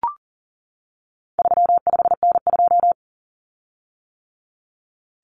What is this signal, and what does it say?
Morse code, heard as a keyed tone, about 40 wpm, 700 Hz. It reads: 35N2
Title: Common Cut Numbers - 4-Digit 40wpm